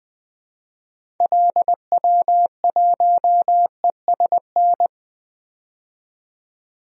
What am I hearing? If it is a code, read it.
Morse code, heard as a keyed tone, about 20 wpm, 700 Hz. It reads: LW1ESN
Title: DXCC-100 Call Signs - 6 Characters - 20wpm